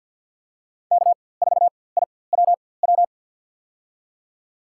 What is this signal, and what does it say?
Morse code, heard as a keyed tone, about 53 wpm, spaced as for 24 wpm, 700 Hz. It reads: K4IWW
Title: USA Call Signs - 5 Characters - 4x Character Spacing - 50wpm